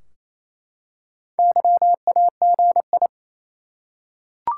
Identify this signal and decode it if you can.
Morse code, heard as a keyed tone, about 28 wpm, 700 Hz. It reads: YAGI
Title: Top 180 QSO Elements 28wpm